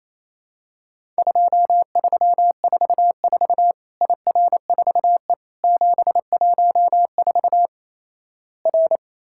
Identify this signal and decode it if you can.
Morse code, heard as a keyed tone, about 28 wpm, 700 Hz. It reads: 2344 IR4E 714 R
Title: Worked All Europe DX Contest - QTC Exchange - 28wpm